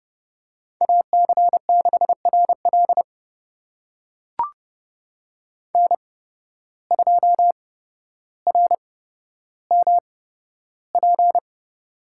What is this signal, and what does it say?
Morse code, heard as a keyed tone, about 31 wpm, spaced as for 13 wpm, 700 Hz. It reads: AC6RL N2RMP
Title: USA Call Signs - 5 Characters - 8x Character Spacing - 30wpm